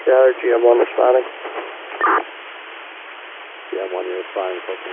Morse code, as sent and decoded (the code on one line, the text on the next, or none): none